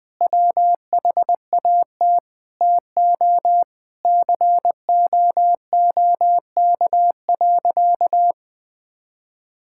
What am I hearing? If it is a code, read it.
Morse code, heard as a keyed tone, about 19 wpm, 700 Hz. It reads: WHAT TO COOK.
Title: Sentences with 5-Word Phrases - 20wpm